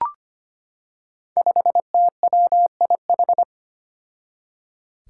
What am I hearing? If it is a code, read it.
Morse code, heard as a keyed tone, about 25 wpm, 700 Hz. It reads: 5TWIH
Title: Five Challenging Letter-Number Combinations 25wpm